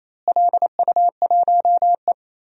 Morse code, heard as a keyed tone, about 28 wpm, 700 Hz. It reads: LU1E
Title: DXCC-100 Call Signs - 4 Characters - Repeated 3x - 28wpm